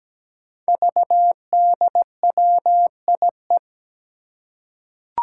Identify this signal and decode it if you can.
Morse code, heard as a keyed tone, about 17 wpm, 700 Hz. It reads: VDWIE
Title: Five Challenging Letter-Number Combinations 17wpm